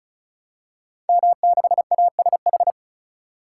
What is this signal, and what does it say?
Morse code, heard as a keyed tone, about 36 wpm, 700 Hz. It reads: M6ASH
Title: Code Groups - 5 Letter:Numbers - 35wpm